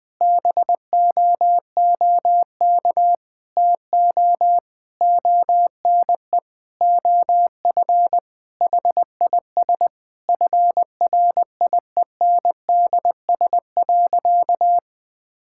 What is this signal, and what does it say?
Morse code, heard as a keyed tone, about 20 wpm, 700 Hz. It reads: BOOK TO ONE OF HIS FRIENDS.
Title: Sentences with 4-Word Phrases - 20wpm